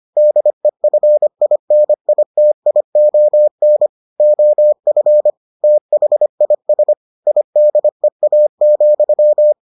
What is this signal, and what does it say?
Morse code, heard as a keyed tone, about 25 wpm, 600 Hz. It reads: DEFINITION OF THIS IDEA,